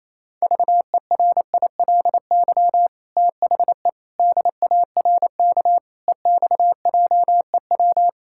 Morse code, heard as a keyed tone, about 28 wpm, 700 Hz. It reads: VERILY THE DARK EXJEW
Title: Pangram Challenge v2 - 28wpm